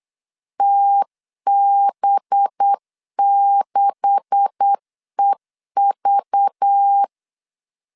Morse code, heard as a keyed tone, about 8 wpm, 800 Hz. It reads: TB6EV
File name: LEZIONI / 10 SIGNS / 10D COMA 8wpm 3749